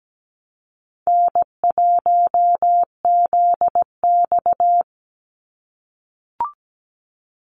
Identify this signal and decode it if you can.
Morse code, heard as a keyed tone, about 17 wpm, 700 Hz. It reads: N1ZX
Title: Easy Call Signs - 17wpm